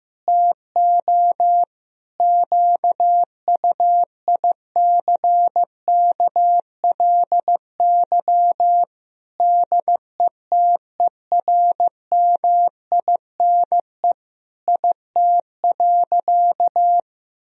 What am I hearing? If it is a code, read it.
Morse code, heard as a keyed tone, about 15 wpm, 700 Hz. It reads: TO QUICKLY DETERMINE IT.
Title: Sentences from Top 1000 Words - Review Part 4 - 15wpm